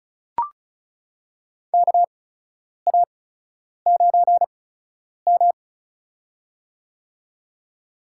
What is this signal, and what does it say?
Morse code, heard as a keyed tone, about 36 wpm, spaced as for 10 wpm, 700 Hz. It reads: KA9M
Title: USA Call Signs - 4 Characters - 8x Character Spacing - 35wpm